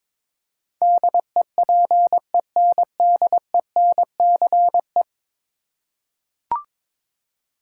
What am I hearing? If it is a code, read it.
Morse code, heard as a keyed tone, about 22 wpm, 700 Hz. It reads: DEPENDENCE
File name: practice / 22wpm / 10 Letter Words - First 3 Letters Given - 22wpm